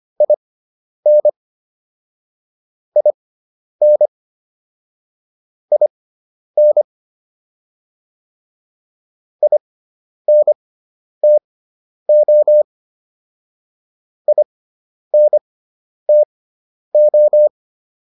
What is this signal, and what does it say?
Morse code, heard as a keyed tone, about 25 wpm, spaced as for 10 wpm, 600 Hz. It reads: IN IN IN INTO INTO